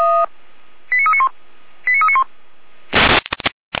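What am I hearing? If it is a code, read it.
Five-tone selective call (ZVEI), heard as quick runs of five tones, tones of about 70 ms. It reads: 89391 89391
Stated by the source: ZVEI five-tone selective call received on an SDR.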